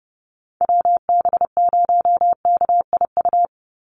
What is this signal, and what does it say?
Morse code, heard as a keyed tone, about 30 wpm, 700 Hz. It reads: WB0KIU
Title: NA SOTA Chaser Callsigns 30wpm